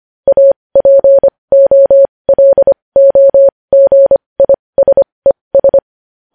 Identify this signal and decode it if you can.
Morse code, heard as a keyed tone, about 25 wpm, spaced as for 20 wpm, 550 Hz. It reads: APOLOGISES
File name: podcast / News headlines-1x-25x20